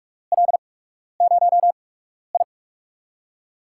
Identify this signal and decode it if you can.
Morse code, heard as a keyed tone, about 43 wpm, 700 Hz. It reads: L 0 I
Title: USA Call Signs - 8x Character Spacing - 45wpm